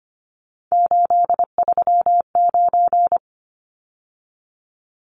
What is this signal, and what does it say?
Morse code, heard as a keyed tone, about 25 wpm, 700 Hz. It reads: 839
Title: Numbers - 3-Digit 25wpm